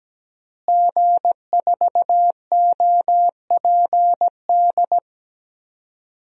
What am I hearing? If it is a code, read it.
Morse code, heard as a keyed tone, about 17 wpm, 700 Hz. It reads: G4OPD
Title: DXCC-100 Call Signs - 5 Characters - Repeated 3x - 17wpm